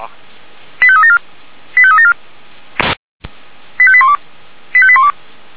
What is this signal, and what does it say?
Five-tone selective call (ZVEI), heard as quick runs of five tones, tones of about 70 ms, with speech. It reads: 85374 85374 85812 85812